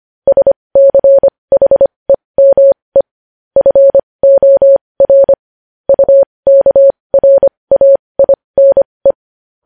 Morse code, heard as a keyed tone, about 25 wpm, spaced as for 20 wpm, 550 Hz. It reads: SCHEME FOR UKRAINE
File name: podcast / News headlines-1x-25x20